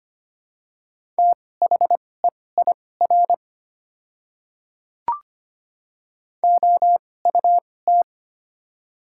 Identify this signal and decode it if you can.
Morse code, heard as a keyed tone, about 25 wpm, spaced as for 18 wpm, 700 Hz. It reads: THEIR OUT
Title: 2x Character Spacing - Top 200 Words - 25wpm